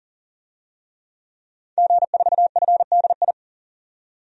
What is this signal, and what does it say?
Morse code, heard as a keyed tone, about 37 wpm, 700 Hz. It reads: G4FDI
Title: DXCC-100 Call Signs - 5 Characters - 40wpm